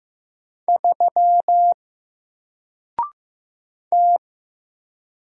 Morse code, heard as a keyed tone, about 15 wpm, 700 Hz. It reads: 3 T
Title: Single Letter-Number 15wpm